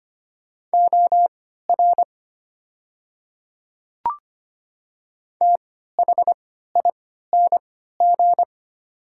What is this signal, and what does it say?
Morse code, heard as a keyed tone, about 25 wpm, spaced as for 14 wpm, 700 Hz. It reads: OR THING
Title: 3x Character Spacing - Top 100 Words - 25wpm